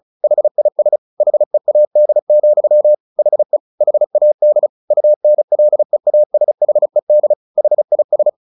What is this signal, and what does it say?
Morse code, heard as a keyed tone, about 34 wpm, 600 Hz. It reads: HIS HEAD, HE HAD UNLEASHED HIS
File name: the old man and the sea 2 35wpm